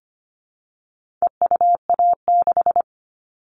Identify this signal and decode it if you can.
Morse code, heard as a keyed tone, about 25 wpm, 700 Hz. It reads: EUA6
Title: All Cut Numbers - 4-Digit 25wpm